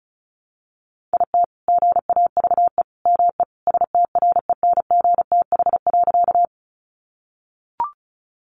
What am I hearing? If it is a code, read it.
Morse code, heard as a keyed tone, about 34 wpm, 700 Hz. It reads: IT GAVE ME STRENGTH.
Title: Sentences from Top 1300 Words 35wpm